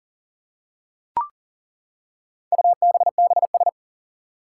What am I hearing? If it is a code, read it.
Morse code, heard as a keyed tone, about 40 wpm, 700 Hz. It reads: UBBS
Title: Four Challenging Letter-Number Combinations 40wpm